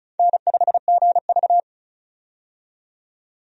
Morse code, heard as a keyed tone, about 36 wpm, 700 Hz. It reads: N5GV
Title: USA Call Signs - 4 Characters - 35wpm